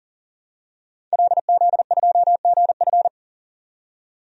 Morse code, heard as a keyed tone, about 37 wpm, 700 Hz. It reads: LZ2GF
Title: DXCC-100 Call Signs - 5 Characters - 40wpm